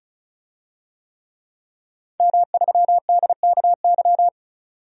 Morse code, heard as a keyed tone, about 36 wpm, 700 Hz. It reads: M3DKY
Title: DXCC-100 Call Signs - 5 Characters - 35wpm